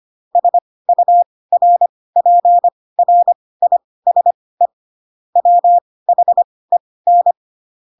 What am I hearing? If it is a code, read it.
Morse code, heard as a keyed tone, about 25 wpm, spaced as for 18 wpm, 700 Hz. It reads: SURPRISE WHEN
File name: Pinocchio ch3 12 25 ik1hge